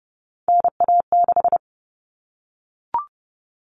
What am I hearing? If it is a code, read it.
Morse code, heard as a keyed tone, about 30 wpm, 700 Hz. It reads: NA6
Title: Common Cut Numbers - 3-Digit 30wpm